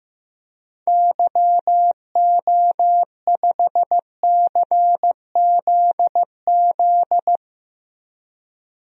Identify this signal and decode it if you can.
Morse code, heard as a keyed tone, about 15 wpm, 700 Hz. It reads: YO5CZZ
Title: DXCC-100 Call Signs - 6 Characters - 15wpm